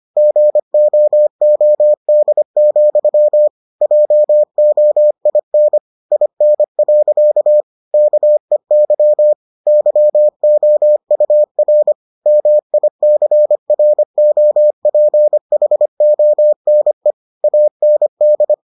Morse code, heard as a keyed tone, about 25 wpm, 600 Hz. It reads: GOOD, JOIN IN. KEY YOUR MICROPHONE AND